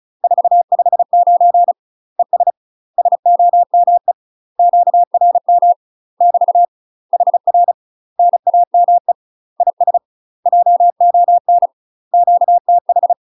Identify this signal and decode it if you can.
Morse code, heard as a keyed tone, about 36 wpm, 700 Hz. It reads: 459 ES SOME QRM <BT> HR NAME IS JON QTH
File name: UK MASTER QSOs TRAINING / 35WPM / UK MASTER QSOs 0041